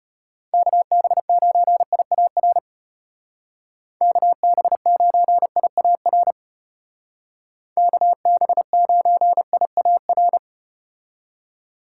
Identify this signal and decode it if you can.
Morse code, heard as a keyed tone, about 34 wpm, 700 Hz. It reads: KB9IAR KB9IAR KB9IAR